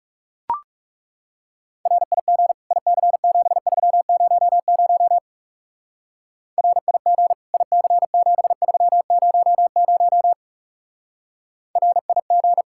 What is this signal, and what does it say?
Morse code, heard as a keyed tone, about 39 wpm, 700 Hz. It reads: RIG IC7300 RIG IC7300 RIG